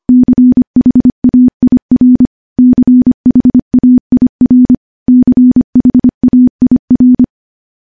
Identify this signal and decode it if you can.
Morse code, heard as a keyed tone, about 25 wpm, 250 Hz. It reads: CHAIR CHAIR CHAIR
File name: morse-training-25WPM